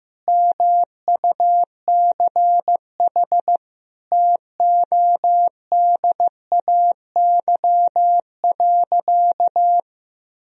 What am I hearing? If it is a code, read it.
Morse code, heard as a keyed tone, about 15 wpm, 700 Hz. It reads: MUCH TODAY.
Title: Sentences from Top 1000 Words - Review Part 4 - 15wpm